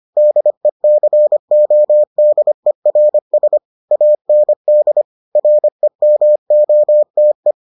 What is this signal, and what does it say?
Morse code, heard as a keyed tone, about 25 wpm, 600 Hz. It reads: DECODERS AND REMOTE